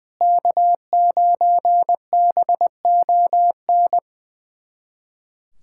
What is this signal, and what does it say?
Morse code, heard as a keyed tone, about 20 wpm, 700 Hz. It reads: K9BON
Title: USA Callsign DE Callsign 20wpm